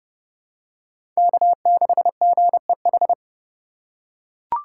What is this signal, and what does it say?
Morse code, heard as a keyed tone, about 30 wpm, 700 Hz. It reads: K6GEH
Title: USA Call Signs - 3x Character Spacing - 30wpm